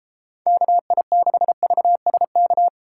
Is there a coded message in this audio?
Yes: Morse code from 0.5 to 2.7 s, heard as a keyed tone, about 29 wpm, 700 Hz. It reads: KI6VSK